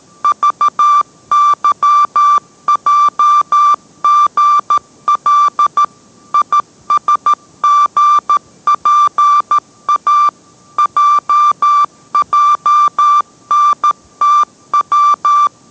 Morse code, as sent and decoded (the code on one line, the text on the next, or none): ...- -.-- .--- --. .-.. / .. ... --. .--. .- / .--- .--- -. - .--
VYJGL ISGPA JJNTW